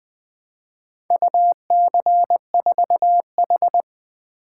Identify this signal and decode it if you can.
Morse code, heard as a keyed tone, about 20 wpm, 700 Hz. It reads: UC4H
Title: Four Letter-Number Combinations - 20wpm